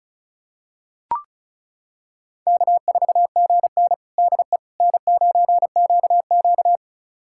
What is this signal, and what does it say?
Morse code, heard as a keyed tone, about 36 wpm, 700 Hz. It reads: K4GN DE N9QQ
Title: USA Callsign DE Callsign - 4 Characters - 35wpm